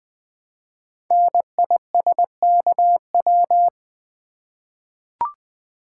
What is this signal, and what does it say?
Morse code, heard as a keyed tone, about 20 wpm, 700 Hz. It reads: NISKW
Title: Code Groups - 5 Letters - 20wpm